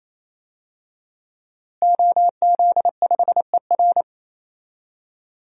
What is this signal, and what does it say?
Morse code, heard as a keyed tone, about 27 wpm, 700 Hz. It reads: OZ5ER
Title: DXCC-100 Call Signs - 5 Characters - Repeated 3x - 28wpm